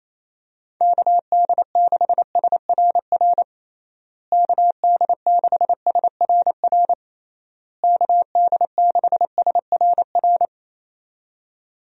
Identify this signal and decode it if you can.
Morse code, heard as a keyed tone, about 28 wpm, 700 Hz. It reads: KD6SRR KD6SRR KD6SRR
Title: USA Call Signs - 6 Characters - Repeated 3x - 28wpm